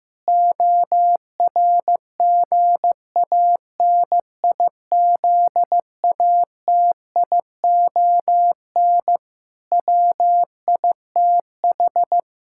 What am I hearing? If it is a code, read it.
Morse code, heard as a keyed tone, about 15 wpm, 700 Hz. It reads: ORGANIZATION WITH
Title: Sentences from Top 1000 Words - Review Part 5 - 15wpm